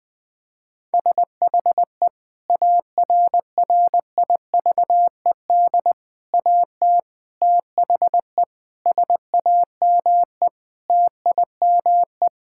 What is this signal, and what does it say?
Morse code, heard as a keyed tone, about 20 wpm, 700 Hz. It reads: SHE ARRIVED AT THE SAME TIME
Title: Sentences with 5-Word Phrases - Spoken Beginning - 20wpm